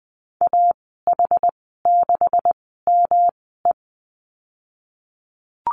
Morse code, heard as a keyed tone, about 20 wpm, spaced as for 15 wpm, 700 Hz. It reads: AH6ME
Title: 128 Lesson 21g - Number 6 - Call Signs 20wpm